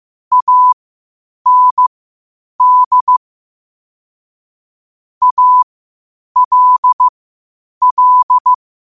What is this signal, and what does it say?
Morse code, heard as a keyed tone, about 15 wpm, spaced as for 8 wpm, 1000 Hz. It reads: AND ALL
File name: qotdfiles / 8wpm